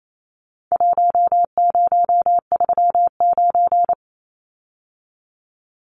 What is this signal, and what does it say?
Morse code, heard as a keyed tone, about 28 wpm, 700 Hz. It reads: 1039
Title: Numbers - Contest Style - 4-Digit 28wpm